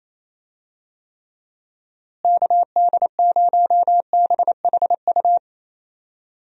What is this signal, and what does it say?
Morse code, heard as a keyed tone, about 28 wpm, 700 Hz. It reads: KD0BHU
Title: USA Call Signs - 6 Characters - Repeated 2x - 28wpm